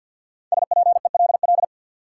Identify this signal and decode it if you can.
Morse code, heard as a keyed tone, about 52 wpm, 700 Hz. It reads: SPELL
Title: Sentences from Top 1000 Words - Review Part 1 - New Word Drill - 50wpm